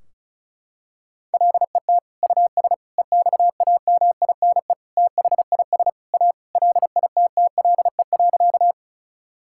Morse code, heard as a keyed tone, about 34 wpm, 700 Hz. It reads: LET US EXAMINE THIS A LITTLE.
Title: Sentences from Top 1600 Words 35wpm